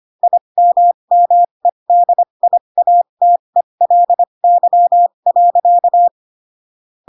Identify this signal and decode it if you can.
Morse code, heard as a keyed tone, about 25 wpm, spaced as for 22 wpm, 700 Hz. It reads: IMMEDIATELY.